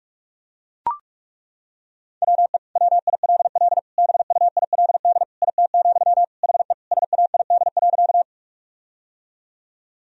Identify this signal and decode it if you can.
Morse code, heard as a keyed tone, about 42 wpm, 700 Hz. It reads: WE WILL BUILD IT, HE SAID.
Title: Sentences from Top 1000 Words - Review Part 1 - 45wpm